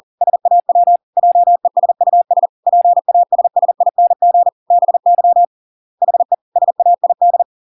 Morse code, heard as a keyed tone, about 40 wpm, 700 Hz. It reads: SAW JESUS PASSING BY HE SAID